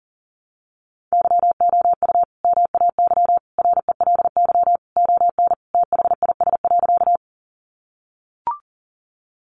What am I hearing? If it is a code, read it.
Morse code, heard as a keyed tone, about 40 wpm, 700 Hz. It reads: YOU MAY RELY ON THIS.
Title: Sentences from Top 1200 Words 40wpm